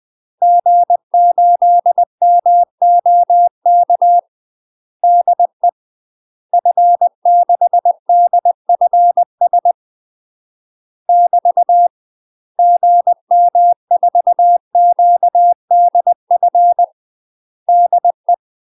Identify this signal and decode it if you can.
Morse code, heard as a keyed tone, about 20 wpm, 700 Hz. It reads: G8MOK DE F6DFS <BT> GM4QDF DE
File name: UK MASTER QSOs TRAINING / 20WPM / UK MASTER QSOs 0021